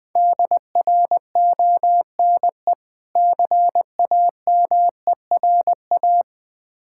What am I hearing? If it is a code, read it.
Morse code, heard as a keyed tone, about 20 wpm, 700 Hz. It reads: DRONE CAMERA
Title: Difficult Words with Hints - 20wpm